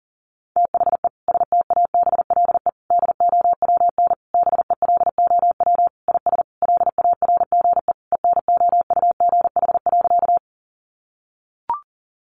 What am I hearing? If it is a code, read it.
Morse code, heard as a keyed tone, about 37 wpm, 700 Hz. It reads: THE STABLE DOWN BELOW IS LARGE ENOUGH.
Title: Sentences from Top 1500 Words 40wpm